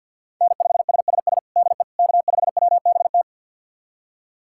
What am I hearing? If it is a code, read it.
Morse code, heard as a keyed tone, about 52 wpm, 700 Hz. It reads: N5SSS DE K5WBT
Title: USA Callsign DE Callsign - 5 Characters - 50wpm